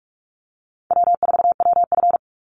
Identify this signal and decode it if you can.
Morse code, heard as a keyed tone, about 44 wpm, 700 Hz. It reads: W4WF
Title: Easy Call Signs Repeated 3x - 45wpm